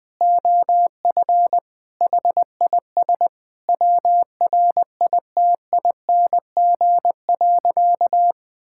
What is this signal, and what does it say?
Morse code, heard as a keyed tone, about 20 wpm, 700 Hz. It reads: OF HIS WRITING.
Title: Sentences with 2-Word Phrases - Spoken Beginning - 20wpm